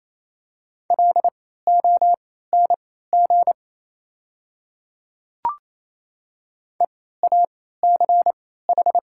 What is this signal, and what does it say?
Morse code, heard as a keyed tone, about 28 wpm, spaced as for 16 wpm, 700 Hz. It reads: LONG EACH
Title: 3x Character Spacing - Top 100 Words - 28wpm